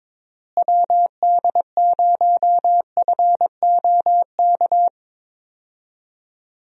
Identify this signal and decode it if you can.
Morse code, heard as a keyed tone, about 22 wpm, 700 Hz. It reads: WD0FOK
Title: USA Call Signs - 6 Characters - Repeated 3x - 22wpm